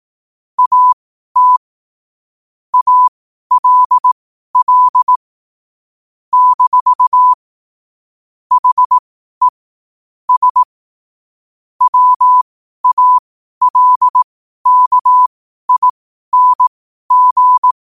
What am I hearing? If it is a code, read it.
Morse code, heard as a keyed tone, about 18 wpm, spaced as for 12 wpm, 1000 Hz. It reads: AT ALL - HE S WALKING
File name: qotdfiles / 12wpm